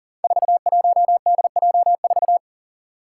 Morse code, heard as a keyed tone, about 40 wpm, 700 Hz. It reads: 41DJ4
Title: Five Challenging Letter-Number Combinations 40wpm